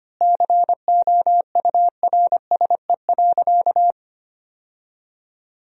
Morse code, heard as a keyed tone, about 25 wpm, 700 Hz. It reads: COURSE.
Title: 3x Word-Spacing - Sentences from Top 500 Words - 25wpm